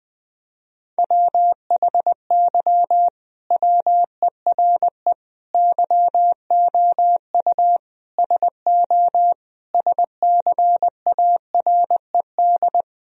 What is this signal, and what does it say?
Morse code, heard as a keyed tone, about 20 wpm, 700 Hz. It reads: WHY WERE YOU SO SCARED
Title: Sentences from Top 1000 Words - Review Part 4 - 20wpm